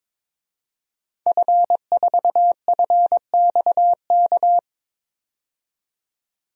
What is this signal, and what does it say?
Morse code, heard as a keyed tone, about 22 wpm, 700 Hz. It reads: F4FXK
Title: DXCC-100 Call Signs - 5 Characters - 22wpm